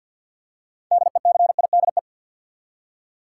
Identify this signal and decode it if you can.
Morse code, heard as a keyed tone, about 51 wpm, 700 Hz. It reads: DECIDE
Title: Six Letter Words - 50wpm